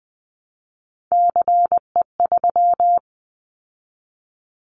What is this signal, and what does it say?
Morse code, heard as a keyed tone, about 20 wpm, 700 Hz. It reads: CE3
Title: Callsign Prefixes for Top 100 DXCC - 20wpm